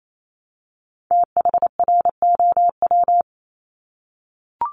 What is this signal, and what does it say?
Morse code, heard as a keyed tone, about 28 wpm, 700 Hz. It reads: THROW
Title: Top 1501-1750 Words 28wpm